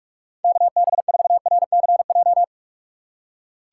Morse code, heard as a keyed tone, about 42 wpm, 700 Hz. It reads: KB4RCJ
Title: USA Call Signs - 6 Characters - 6x Character Spacing - 45wpm